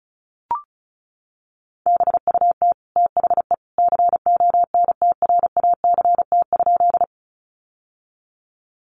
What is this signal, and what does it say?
Morse code, heard as a keyed tone, about 34 wpm, 700 Hz. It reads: BUT THE CONTRACT?
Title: Sentences from Top 1400 Words 35wpm